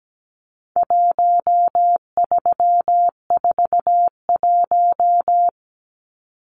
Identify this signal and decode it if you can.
Morse code, heard as a keyed tone, about 17 wpm, 700 Hz. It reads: 1341
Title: Numbers - 4-Digit 17wpm